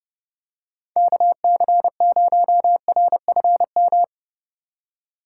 Morse code, heard as a keyed tone, about 30 wpm, 700 Hz. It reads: KC0RFM